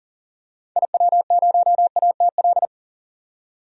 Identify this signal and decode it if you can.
Morse code, heard as a keyed tone, about 40 wpm, 700 Hz. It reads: IW0ATL
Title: DXCC-100 Call Signs - 6 Characters - 40wpm